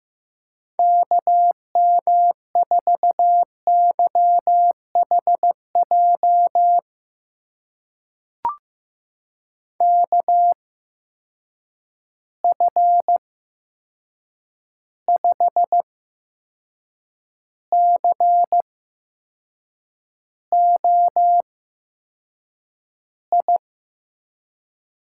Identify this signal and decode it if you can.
Morse code, heard as a keyed tone, about 15 wpm, spaced as for 6 wpm, 700 Hz. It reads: KM4YHJ KF5COI
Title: USA Call Signs - 6 Characters - 8x Character Spacing - 15wpm